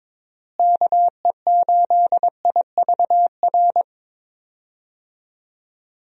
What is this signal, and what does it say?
Morse code, heard as a keyed tone, about 22 wpm, 700 Hz. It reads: KE8IVR
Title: USA Call Signs - 6 Characters - Repeated 2x - 22wpm